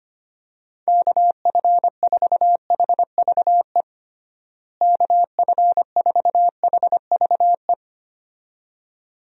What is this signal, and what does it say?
Morse code, heard as a keyed tone, about 25 wpm, 700 Hz. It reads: KF4HVE KF4HVE
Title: USA Call Signs - 6 Characters - Repeated 2x - 25wpm